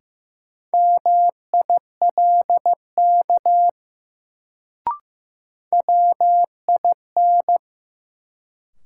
Morse code, heard as a keyed tone, about 15 wpm, 700 Hz. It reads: MILK WIN
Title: Sentences from Top 1500 Words - New Word Drill - 15wpm